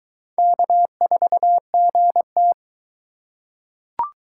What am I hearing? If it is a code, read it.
Morse code, heard as a keyed tone, about 22 wpm, 700 Hz. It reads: K4GT